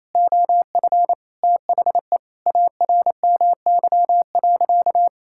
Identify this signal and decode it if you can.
Morse code, heard as a keyed tone, about 28 wpm, 700 Hz. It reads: OF THE ARMY.
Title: Sentences from Top 1000 Words - Review Part 6 - 28wpm